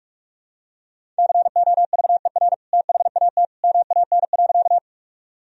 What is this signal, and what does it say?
Morse code, heard as a keyed tone, about 44 wpm, 700 Hz. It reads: COVER THAT MAN.
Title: Sentences from Top 1000 Words - Review Part 1 - 45wpm